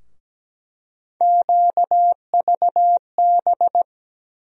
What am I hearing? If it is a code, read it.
Morse code, heard as a keyed tone, about 17 wpm, 700 Hz. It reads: QVB
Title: USA Call Sign Postfixes - 3 Letters 17wpm